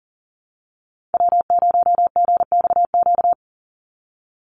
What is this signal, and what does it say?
Morse code, heard as a keyed tone, about 40 wpm, 700 Hz. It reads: W0GXQ
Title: Call Signs - 40wpm